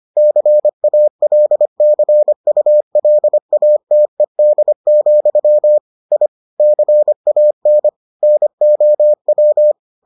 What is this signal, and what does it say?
Morse code, heard as a keyed tone, about 25 wpm, 600 Hz. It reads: CALCULATED, I CAN NOW